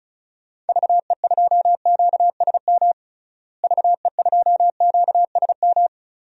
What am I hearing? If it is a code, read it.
Morse code, heard as a keyed tone, about 34 wpm, 700 Hz. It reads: VE2QSM VE2QSM
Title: DXCC-100 Call Signs - 6 Characters - Repeated 2x - 35wpm